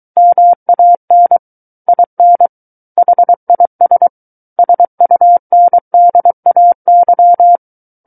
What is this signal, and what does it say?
Morse code, heard as a keyed tone, about 23 wpm, 700 Hz. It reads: MAN IN HIS SUNDAY
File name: Book One / 14-Chapter 14 - 23 WPM